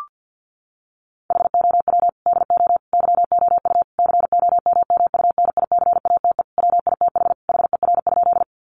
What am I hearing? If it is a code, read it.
Morse code, heard as a keyed tone, about 50 wpm, 700 Hz. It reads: HOW DO YOU COMMUNICATE WITH HER?
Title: Sentences from Top 1300 Words 50wpm